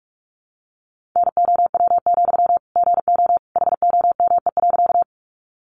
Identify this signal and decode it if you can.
Morse code, heard as a keyed tone, about 44 wpm, 700 Hz. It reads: NOW, GO HOME.
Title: Sentences from Top 200 Words 45wpm